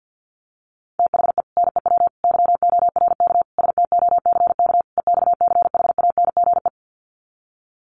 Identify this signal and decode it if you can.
Morse code, heard as a keyed tone, about 50 wpm, 700 Hz. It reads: THE NEW YORK STOCK EXCHANGE
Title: Sets of 5 Words 50wpm